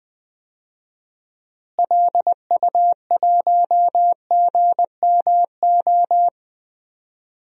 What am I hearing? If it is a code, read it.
Morse code, heard as a keyed tone, about 21 wpm, 700 Hz. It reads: LU1GMO
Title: DXCC-100 Call Signs - 6 Characters - Repeated 2x - 20wpm